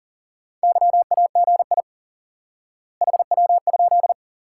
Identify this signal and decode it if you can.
Morse code, heard as a keyed tone, about 40 wpm, 700 Hz. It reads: YAGI HW?
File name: IQE Course 3 - Lesson 89 - YAGI - Two Elements - 40wpm